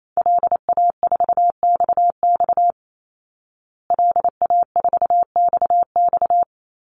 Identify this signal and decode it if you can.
Morse code, heard as a keyed tone, about 28 wpm, 690 Hz. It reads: LA4XX LA4XX
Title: Call Signs Repeated 3x - 28wpm